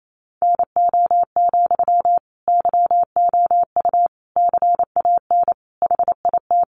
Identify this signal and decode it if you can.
Morse code, heard as a keyed tone, about 28 wpm, 700 Hz. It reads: NO, YOU CAN HIT
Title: Sentences from Top 1200 Words 28wpm